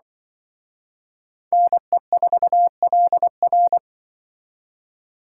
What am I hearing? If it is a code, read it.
Morse code, heard as a keyed tone, about 24 wpm, 700 Hz. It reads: NE4LR
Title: Speed Racing - USA 2x2 Call Signs 20wpm